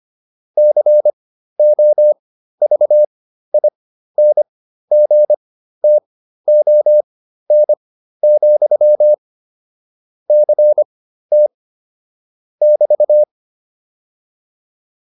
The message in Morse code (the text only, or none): COVINGTON, CT <BT>